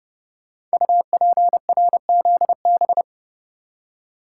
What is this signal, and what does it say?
Morse code, heard as a keyed tone, about 28 wpm, 700 Hz. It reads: UPRZB
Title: Code Groups - 5 Letters - 30wpm